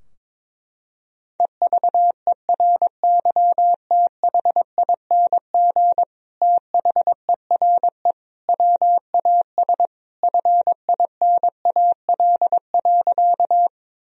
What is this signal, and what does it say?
Morse code, heard as a keyed tone, about 22 wpm, 700 Hz. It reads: EVERYTHING THERE WAS FINAL.
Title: Sentences from Top 1000 Words - Review Part 2 - 22wpm